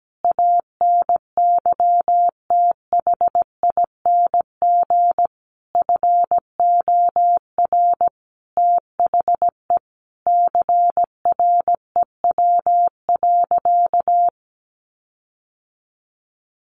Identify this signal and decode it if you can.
Morse code, heard as a keyed tone, about 17 wpm, 700 Hz. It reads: ANYTHING FOR THE CREW.
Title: Sentences from Top 1500 Words 17wpm